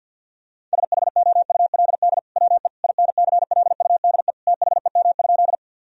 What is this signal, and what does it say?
Morse code, heard as a keyed tone, about 50 wpm, 700 Hz. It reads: SHOULD WE INCLUDE THEM?
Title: Sentences from Top 1000 Words - Review Part 3 - 50wpm